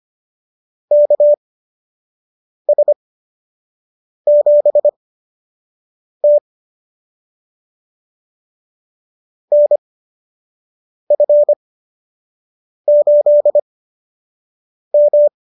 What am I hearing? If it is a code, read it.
Morse code, heard as a keyed tone, about 25 wpm, spaced as for 6 wpm, 590 Hz. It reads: KS7T NF8M